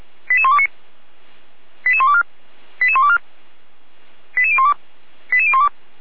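Five-tone selective call (ZVEI), heard as quick runs of five tones, tones of about 70 ms. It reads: 88129 88124 88124 88012 88012
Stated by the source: ZVEI five-tone call received on SDR, recorded by Feuerwehr Orlamünde.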